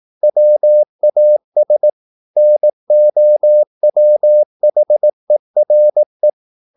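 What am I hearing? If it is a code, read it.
Morse code, heard as a keyed tone, about 18 wpm, 600 Hz. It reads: WAS NOWHERE
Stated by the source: Morse practice at 18 wpm, English text, from CW med Gnister.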